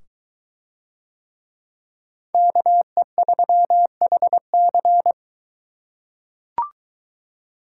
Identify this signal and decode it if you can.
Morse code, heard as a keyed tone, about 21 wpm, 700 Hz. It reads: KE3HC